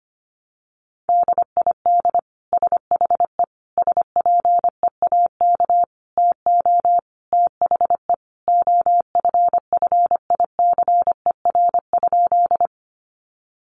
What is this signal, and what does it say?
Morse code, heard as a keyed tone, about 25 wpm, 700 Hz. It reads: DID SHE SPEAK TO THE OFFICER?
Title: Sentences from Top 1200 Words 25wpm